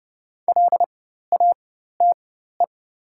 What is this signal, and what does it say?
Morse code, heard as a keyed tone, about 29 wpm, spaced as for 14 wpm, 690 Hz. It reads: LATE
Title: 4x Character Spacing - Top 500 Words - 30wpm